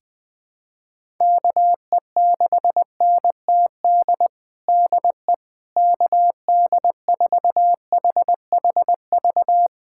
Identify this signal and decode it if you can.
Morse code, heard as a keyed tone, about 20 wpm, 700 Hz. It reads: KE6NTD DE KD4HHV
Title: USA Callsign DE Callsign 20wpm